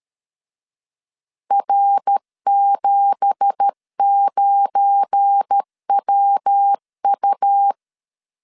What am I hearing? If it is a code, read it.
Morse code, heard as a keyed tone, about 13 wpm, 800 Hz. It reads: R79WU
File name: CW / DET / 1 9 1500C S C 12wpm 3811